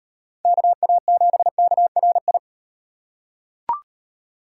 Morse code, heard as a keyed tone, about 36 wpm, 700 Hz. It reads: KA7KRI